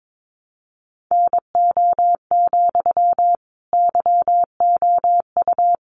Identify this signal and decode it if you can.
Morse code, heard as a keyed tone, about 22 wpm, 700 Hz. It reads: NO, YOU
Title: Sentences from Top 1200 Words 22wpm